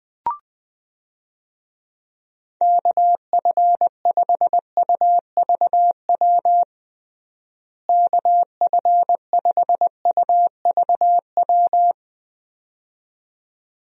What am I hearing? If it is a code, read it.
Morse code, heard as a keyed tone, about 20 wpm, 700 Hz. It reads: KF5UVW KF5UVW
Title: USA Call Signs - 6 Characters - Repeated 2x - 20wpm